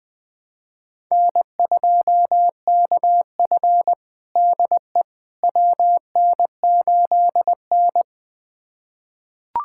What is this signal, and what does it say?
Morse code, heard as a keyed tone, about 20 wpm, 700 Hz. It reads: N2KF DE WN8N
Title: USA Callsign DE Callsign - 4 Characters - 20wpm